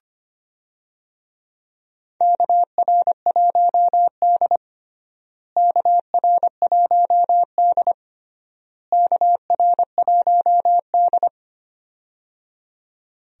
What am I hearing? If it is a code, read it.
Morse code, heard as a keyed tone, about 25 wpm, 700 Hz. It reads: KR1D KR1D KR1D
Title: USA Call Signs - 4 Characters - Repeated 3x - 25wpm